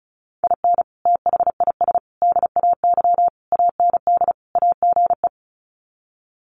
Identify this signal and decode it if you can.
Morse code, heard as a keyed tone, about 34 wpm, 700 Hz. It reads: IN THIS DAY AND AGE
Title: Sets of 5 Words Encore 35wpm